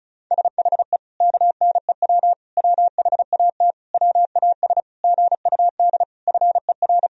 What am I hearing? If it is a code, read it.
Morse code, heard as a keyed tone, about 36 wpm, 700 Hz. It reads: SHE KNEW WHAT WAS GUD FER
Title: Sentences from Top 400 Words and Abbreviations 35wpm